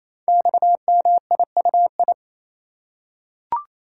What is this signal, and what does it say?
Morse code, heard as a keyed tone, about 29 wpm, 700 Hz. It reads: XMIUI
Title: Code Groups - 5 Letters - 28wpm